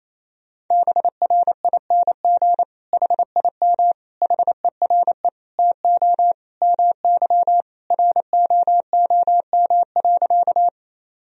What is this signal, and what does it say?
Morse code, heard as a keyed tone, about 28 wpm, 700 Hz. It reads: BRING HIM HERE TO MY ROOM.
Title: Sentences from Top 1000 Words - Review Part 3 - 28wpm